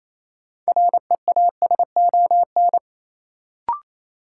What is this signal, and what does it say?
Morse code, heard as a keyed tone, about 28 wpm, 700 Hz. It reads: REASON
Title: Sets of 5 Words Encore - New Word Drill - 28wpm